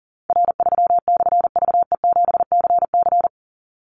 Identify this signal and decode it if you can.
Morse code, heard as a keyed tone, about 40 wpm, 700 Hz. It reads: R3/VE7CC
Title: Call Signs - 40wpm